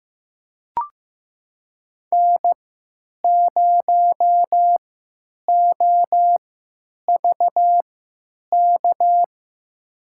Morse code, heard as a keyed tone, about 15 wpm, spaced as for 9 wpm, 700 Hz. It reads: N0OVK
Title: USA Call Signs - 5 Characters - 3x Character Spacing - 15wpm